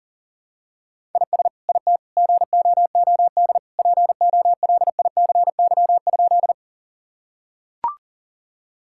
Morse code, heard as a keyed tone, about 37 wpm, 700 Hz. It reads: IS IT GOOD POLICY?
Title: Sentences from Top 1000 Words - Review Part 5 - 40wpm